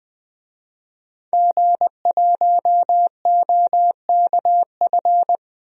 Morse code, heard as a keyed tone, about 20 wpm, 700 Hz. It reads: G1OKF
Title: DXCC-100 Call Signs - 5 Characters - 20wpm